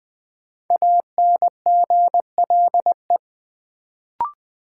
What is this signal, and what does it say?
Morse code, heard as a keyed tone, about 20 wpm, 700 Hz. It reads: ANGLE
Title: Sentences from Top 1500 Words - New Word Drill Alt - 20wpm